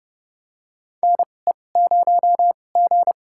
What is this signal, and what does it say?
Morse code, heard as a keyed tone, about 32 wpm, spaced as for 22 wpm, 700 Hz. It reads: NE0G
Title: USA Call Signs - 4 Characters - 2x Character Spacing - 30wpm